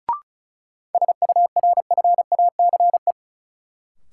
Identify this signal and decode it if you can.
Morse code, heard as a keyed tone, about 36 wpm, 700 Hz. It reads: SURFACE
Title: Sentences from Top 600 Words - New Word Drill - 35wpm